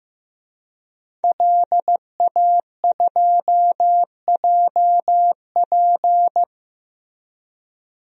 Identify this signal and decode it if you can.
Morse code, heard as a keyed tone, about 15 wpm, 700 Hz. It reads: LA2JP
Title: DXCC-100 Call Signs - 5 Characters - 15wpm